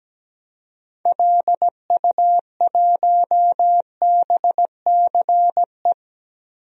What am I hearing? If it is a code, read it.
Morse code, heard as a keyed tone, about 17 wpm, 700 Hz. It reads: LU1BCE
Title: DXCC-100 Call Signs - 6 Characters - 17wpm